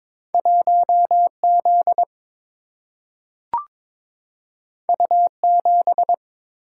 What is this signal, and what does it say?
Morse code, heard as a keyed tone, about 22 wpm, 700 Hz. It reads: 1 Z U 7
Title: Single-Letter Single-Number 22wpm